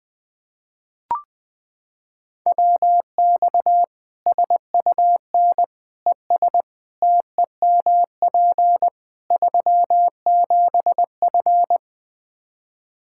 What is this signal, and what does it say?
Morse code, heard as a keyed tone, about 21 wpm, 700 Hz. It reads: WX SUN ES TEMP 37F
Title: K1USN SST QSO Practice 20wpm